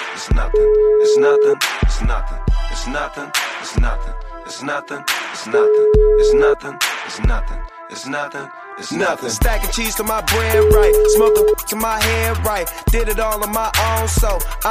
Ringback tone (Europe, 425 Hz), with noise.